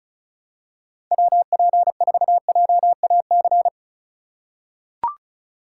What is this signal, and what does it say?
Morse code, heard as a keyed tone, about 34 wpm, 700 Hz. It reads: WP4JAC
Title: USA Call Signs - 6 Characters - 3x Character Spacing - 35wpm